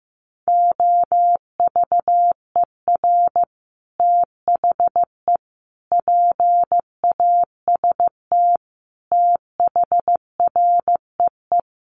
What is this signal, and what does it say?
Morse code, heard as a keyed tone, about 15 wpm, 700 Hz. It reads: OVER THE PAST THREE
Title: Sets of 5 Words Encore2 15wpm